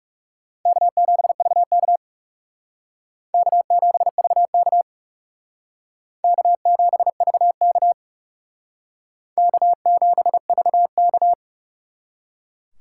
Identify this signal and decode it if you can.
Morse code, heard as a keyed tone, about 37 wpm, 700 Hz. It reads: K7VK K7VK K7VK K7VK
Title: Speed Racing - USA Call Signs 30wpm